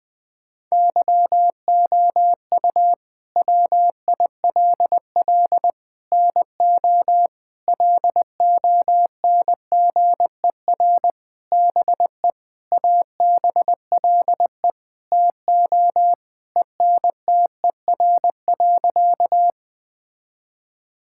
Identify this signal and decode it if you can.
Morse code, heard as a keyed tone, about 21 wpm, 700 Hz. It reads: YOU WILL NO LONGER BE ABLE TO ENTER.
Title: Sentences with 6-Word Phrases - 20wpm